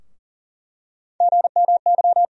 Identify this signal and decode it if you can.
Morse code, heard as a keyed tone, about 40 wpm, 700 Hz. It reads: GMY
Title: USA Call Sign Postfixes - 3 Letters 40wpm